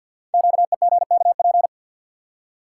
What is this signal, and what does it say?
Morse code, heard as a keyed tone, about 51 wpm, 690 Hz. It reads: QEGKP
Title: Code Groups - 5 Letters - 50wpm